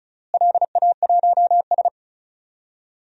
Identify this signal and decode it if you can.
Morse code, heard as a keyed tone, about 34 wpm, 700 Hz. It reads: LA1S
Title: DXCC-100 Call Signs - 4 Characters - Repeated 2x - 35wpm